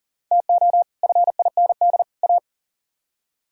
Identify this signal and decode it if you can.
Morse code, heard as a keyed tone, about 52 wpm, 700 Hz. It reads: TO FIND A